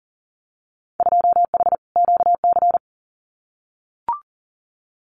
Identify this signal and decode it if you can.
Morse code, heard as a keyed tone, about 42 wpm, 700 Hz. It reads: 2H QC
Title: Winter Field Day Exchanges Encore 40wpm